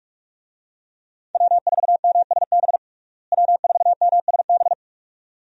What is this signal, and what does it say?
Morse code, heard as a keyed tone, about 41 wpm, 700 Hz. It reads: W4MSB W4MSB
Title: USA Call Signs - 5 Characters - Repeated 2x - 45wpm